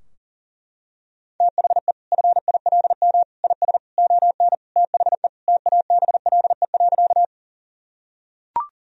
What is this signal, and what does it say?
Morse code, heard as a keyed tone, about 40 wpm, 700 Hz. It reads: THE FILM IS ON THE TABLE.
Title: Sentences from Top 1000 Words - Review Part 3 - 40wpm